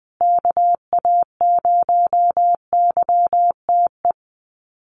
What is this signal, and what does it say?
Morse code, heard as a keyed tone, about 20 wpm, 700 Hz. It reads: KA0YTE
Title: USA Call Signs Repeated 2x 20wpm